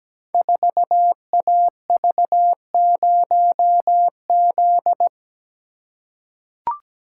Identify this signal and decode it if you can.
Morse code, heard as a keyed tone, about 17 wpm, 690 Hz. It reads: 4AV0Z
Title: Code Groups - 5 Letter:Numbers - 17wpm